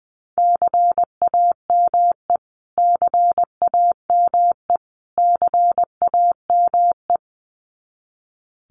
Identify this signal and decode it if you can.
Morse code, heard as a keyed tone, about 20 wpm, 700 Hz. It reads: CAME CAME CAME
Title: Top 200 Words Repeated 20wpm